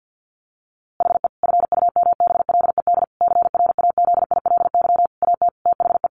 Morse code, heard as a keyed tone, about 50 wpm, 700 Hz. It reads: HE FUMBLED CRAZILY AT THE
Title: A Sound Of Thunder 50wpm